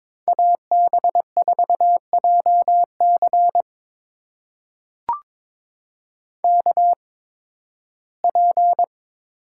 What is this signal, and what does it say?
Morse code, heard as a keyed tone, about 23 wpm, spaced as for 15 wpm, 700 Hz. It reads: AB4JC KP
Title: USA Call Signs - 5 Characters - 8x Character Spacing - 22wpm